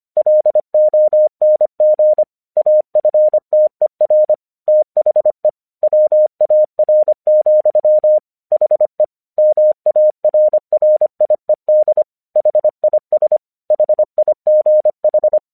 Morse code, heard as a keyed tone, about 25 wpm, 610 Hz. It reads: LONG AFTER THE WAR, HE MARRIED HIS HIGH
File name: 1 English 25wpm / AQC10